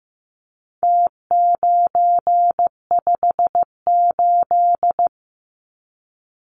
Morse code, heard as a keyed tone, about 15 wpm, 700 Hz. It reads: T958
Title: Numbers - Contest Style - 4-Digit 15wpm